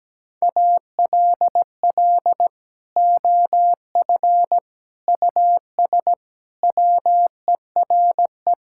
Morse code, heard as a keyed tone, about 17 wpm, 700 Hz. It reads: ALL OF US WERE
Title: Sentences from Top 1000 Words - Review Part 5 - 17wpm